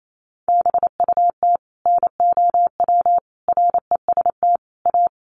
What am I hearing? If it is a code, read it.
Morse code, heard as a keyed tone, about 28 wpm, 700 Hz. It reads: BUT NOW REST A
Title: Sentences from Top 1100 Words 28wpm